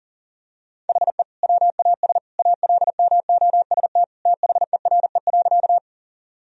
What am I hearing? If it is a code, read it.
Morse code, heard as a keyed tone, about 40 wpm, 700 Hz. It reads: HE WAS ALMOST THERE.
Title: Sentences from Top 1000 Words - Review Part 4 - 40wpm